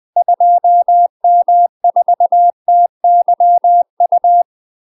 Morse code, heard as a keyed tone, about 20 wpm, 700 Hz. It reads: 2M4TYU
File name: UK MASTER QSOs TRAINING / 20WPM / UK MASTER QSOs 0004